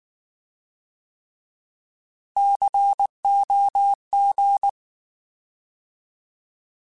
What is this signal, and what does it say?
Morse code, heard as a keyed tone, about 19 wpm, 800 Hz. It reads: COG